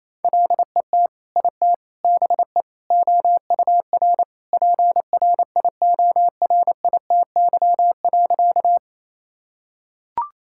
Morse code, heard as a keyed tone, about 28 wpm, 700 Hz. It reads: LET IT BE OUR PRIORITY.
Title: Sentences from Top 1600 Words 28wpm